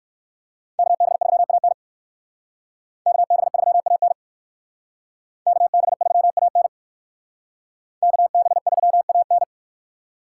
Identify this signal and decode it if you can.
Morse code, heard as a keyed tone, about 63 wpm, 700 Hz. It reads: KB3AN KB3AN KB3AN KB3AN